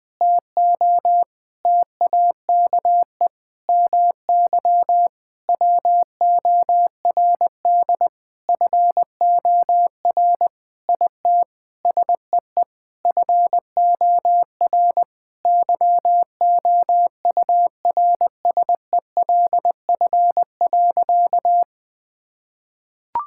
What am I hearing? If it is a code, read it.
Morse code, heard as a keyed tone, about 21 wpm, 700 Hz. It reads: TO TAKE MY WORD FOR IT SEE FOR YOURSELF.
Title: Sentences with 8-Word Phrases - Spoken Beginning - 20wpm